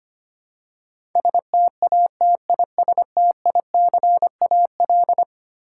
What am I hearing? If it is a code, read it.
Morse code, heard as a keyed tone, about 25 wpm, 700 Hz. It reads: STATISTICAL
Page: morse-code-ninja-practice-25wpm